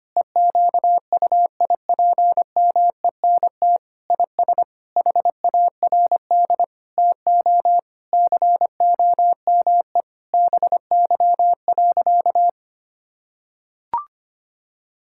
Morse code, heard as a keyed tone, about 25 wpm, 700 Hz. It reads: EQUIPMENT IS HARD TO COME BY.
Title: Sentences from Top 1000 Words - Review Part 5 - 25wpm